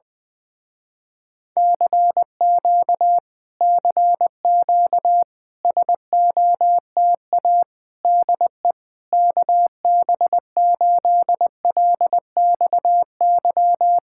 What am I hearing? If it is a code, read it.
Morse code, heard as a keyed tone, about 20 wpm, 700 Hz. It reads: CQ CQ SOTA DE KB8LXY
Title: Speed-Racing Common QSO Phrases Encore - 17wpm